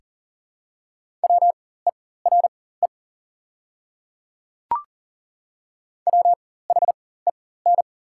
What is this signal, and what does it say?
Morse code, heard as a keyed tone, about 40 wpm, spaced as for 19 wpm, 700 Hz. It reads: WERE WHEN
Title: 4x Character Spacing - Top 500 Words - 40wpm